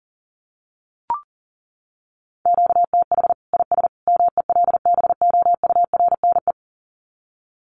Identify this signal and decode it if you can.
Morse code, heard as a keyed tone, about 40 wpm, 700 Hz. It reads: QTH IS MELBOURNE
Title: Common QSO Phrases 40wpm